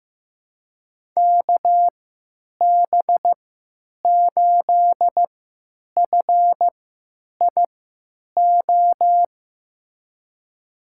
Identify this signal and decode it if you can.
Morse code, heard as a keyed tone, about 15 wpm, spaced as for 9 wpm, 700 Hz. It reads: KB8FIO
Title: USA Call Signs - 6 Characters - 3x Character Spacing - 15wpm